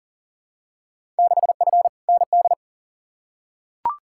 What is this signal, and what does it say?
Morse code, heard as a keyed tone, about 40 wpm, 700 Hz. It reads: 6F ND
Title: Field Day Exchanges Encore - 40wpm